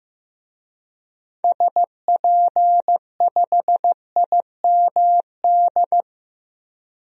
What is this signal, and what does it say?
Morse code, heard as a keyed tone, about 15 wpm, 700 Hz. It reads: SP5IMD
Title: DXCC-100 Call Signs - 6 Characters - 15wpm